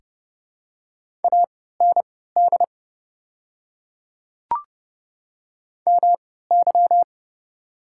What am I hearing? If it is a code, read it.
Morse code, heard as a keyed tone, about 32 wpm, spaced as for 17 wpm, 700 Hz. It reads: AND MY
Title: 3x Character Spacing - Top 100 Words - 30wpm